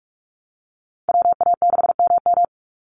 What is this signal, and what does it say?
Morse code, heard as a keyed tone, about 44 wpm, 700 Hz. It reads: WA6MM
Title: NA SOTA Activator Callsigns 45wpm